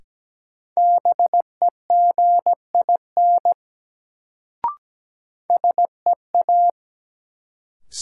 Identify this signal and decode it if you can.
Morse code, heard as a keyed tone, about 17 wpm, 700 Hz. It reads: BEGIN SEA
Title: Sentences from Top 300 Words - New Word Drill - 17wpm